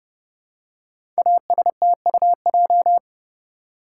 Morse code, heard as a keyed tone, about 30 wpm, 700 Hz. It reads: ASTUJ
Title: Code Groups - 5 Letters - 30wpm